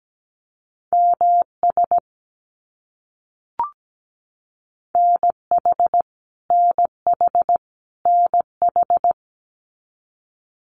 Morse code, heard as a keyed tone, about 17 wpm, 700 Hz. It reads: MS NH NH NH
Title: US States Repeated 3x 17wpm